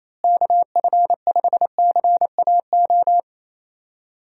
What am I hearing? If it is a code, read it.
Morse code, heard as a keyed tone, about 28 wpm, 700 Hz. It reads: KF5CAO
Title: USA Call Signs - 6 Characters - Repeated 3x - 28wpm